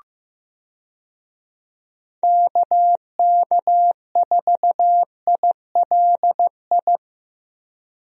Morse code, heard as a keyed tone, about 16 wpm, 700 Hz. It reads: KK4ILI
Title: USA Call Signs - 6 Characters - Repeated 3x - 15wpm